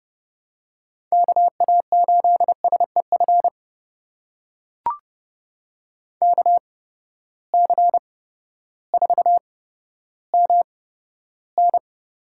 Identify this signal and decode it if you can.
Morse code, heard as a keyed tone, about 30 wpm, spaced as for 14 wpm, 700 Hz. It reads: KA8SEF KC4MN
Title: USA Call Signs - 8x Character Spacing - 30wpm